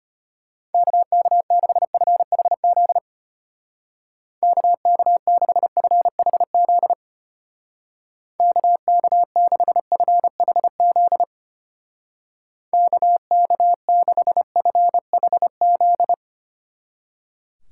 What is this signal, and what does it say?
Morse code, heard as a keyed tone, about 32 wpm, 700 Hz. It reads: KK6FHZ KK6FHZ KK6FHZ KK6FHZ